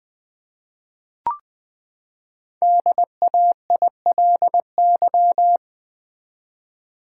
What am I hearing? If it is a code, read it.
Morse code, heard as a keyed tone, about 20 wpm, 700 Hz. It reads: DAILY
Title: Sentences from Top 1100 Words - New Word Drill v2 - 20wpm